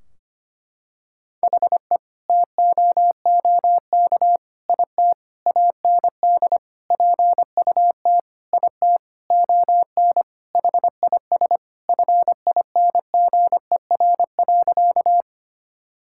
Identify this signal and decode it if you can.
Morse code, heard as a keyed tone, about 25 wpm, 700 Hz. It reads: HE TOOK IT AND PUT IT ON HIS FINGER.
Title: Sentences from Top 1600 Words 25wpm